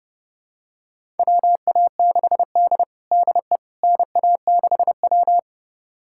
Morse code, heard as a keyed tone, about 29 wpm, 700 Hz. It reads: WA6D DE NA6W
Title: USA Callsign DE Callsign - 4 Characters - 30wpm